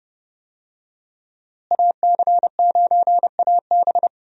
Morse code, heard as a keyed tone, about 30 wpm, 700 Hz. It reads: AC9AB
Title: USA Call Signs - 5 Characters - Repeated 2x - 30wpm